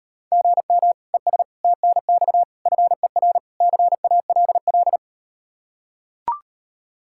Morse code, heard as a keyed tone, about 38 wpm, 700 Hz. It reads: GM ES TNX FER CALL